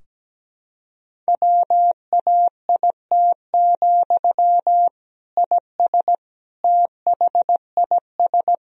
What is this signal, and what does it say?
Morse code, heard as a keyed tone, about 17 wpm, 700 Hz. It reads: WAIT, IS THIS
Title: Sentences from Top 1000 Words - Review Part 4 - 17wpm